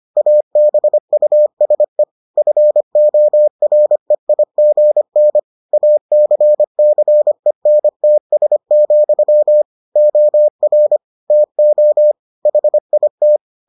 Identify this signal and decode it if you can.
Morse code, heard as a keyed tone, about 25 wpm, 600 Hz. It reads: ABUSE FOREIGN ACCENTS, OR TO HIT